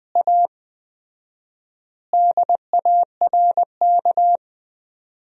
Morse code, heard as a keyed tone, about 20 wpm, 700 Hz. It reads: A DARK
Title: 4x Word-Spacing - Sentences from Top 500 Words - 20wpm